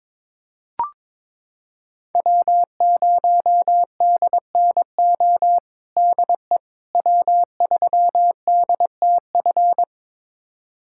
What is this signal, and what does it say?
Morse code, heard as a keyed tone, about 22 wpm, 700 Hz. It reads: W0DNO DE W3DTF
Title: USA Callsign DE Callsign - 5 Characters - 22wpm